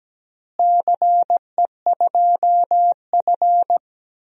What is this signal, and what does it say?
Morse code, heard as a keyed tone, about 17 wpm, 700 Hz. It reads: CE2F
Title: DXCC-100 Call Signs - 4 Characters - Repeated 3x - 17wpm